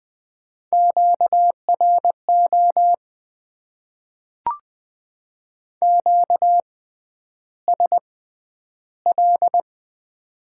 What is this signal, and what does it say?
Morse code, heard as a keyed tone, about 20 wpm, spaced as for 10 wpm, 700 Hz. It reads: QRO QSL
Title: Top 90 QSO Elements - 6x Character Spacing - 20wpm